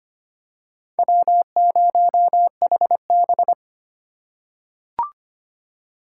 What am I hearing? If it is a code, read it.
Morse code, heard as a keyed tone, about 25 wpm, 700 Hz. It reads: W0HB
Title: USA Call Signs - 4 Characters - 4x Character Spacing - 25wpm